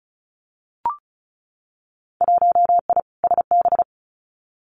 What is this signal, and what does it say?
Morse code, heard as a keyed tone, about 36 wpm, 700 Hz. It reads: 1I SB
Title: Winter Field Day Exchanges Encore 35wpm